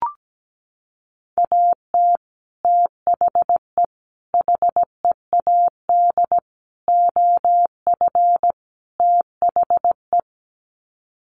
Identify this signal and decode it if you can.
Morse code, heard as a keyed tone, about 17 wpm, 700 Hz. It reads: AT THE HEAD OF THE
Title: Sets of 5 Words 17wpm